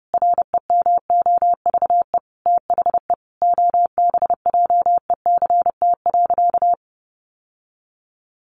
Morse code, heard as a keyed tone, about 30 wpm, 700 Hz. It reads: REMOVE THE OBJECT.
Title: Sentences from Top 800 Words 30wpm